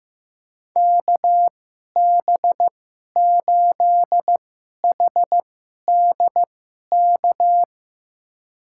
Morse code, heard as a keyed tone, about 15 wpm, spaced as for 11 wpm, 700 Hz. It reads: KB8HDK
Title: USA Call Signs - 6 Characters - 2x Character Spacing - 15wpm